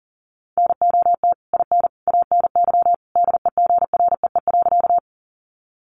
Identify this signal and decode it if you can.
Morse code, heard as a keyed tone, about 40 wpm, 700 Hz. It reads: NOT IN ANY DEGREE.
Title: Sentences from Top 1500 Words 40wpm